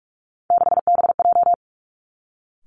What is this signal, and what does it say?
Morse code, heard as a keyed tone, about 46 wpm, 700 Hz. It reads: 6BJ
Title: Three Challenging Letter-Number Combinations 45wpm